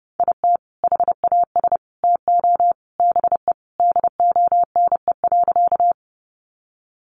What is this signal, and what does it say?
Morse code, heard as a keyed tone, about 30 wpm, 700 Hz. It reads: IT HAS TO BE DONE.
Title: Sentences from Top 500 Words 30wpm